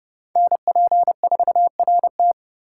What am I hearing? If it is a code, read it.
Morse code, heard as a keyed tone, about 36 wpm, 700 Hz. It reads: NP4RT
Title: Speed Racing - USA Call Signs 30wpm